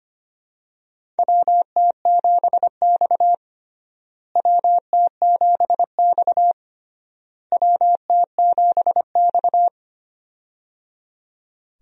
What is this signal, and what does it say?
Morse code, heard as a keyed tone, about 25 wpm, 700 Hz. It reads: WT7X WT7X WT7X
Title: USA Call Signs - 4 Characters - Repeated 3x - 25wpm